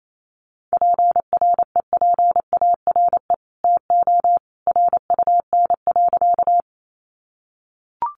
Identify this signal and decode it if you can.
Morse code, heard as a keyed tone, about 28 wpm, 690 Hz. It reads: PREPARE TO RUN.
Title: Sentences from Top 900 Words 28wpm